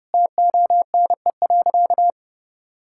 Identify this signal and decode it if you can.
Morse code, heard as a keyed tone, about 30 wpm, 700 Hz. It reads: TONE.
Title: Sentences from Top 1600 Words 30wpm